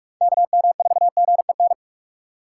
Morse code, heard as a keyed tone, about 46 wpm, 700 Hz. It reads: KM4GEN
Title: USA Call Signs - 6 Characters - 4x Character Spacing - 45wpm